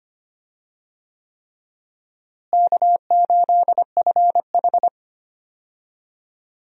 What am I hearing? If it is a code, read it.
Morse code, heard as a keyed tone, about 25 wpm, 700 Hz. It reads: K8FH
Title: USA Call Signs - 4 Characters - Repeated 3x - 25wpm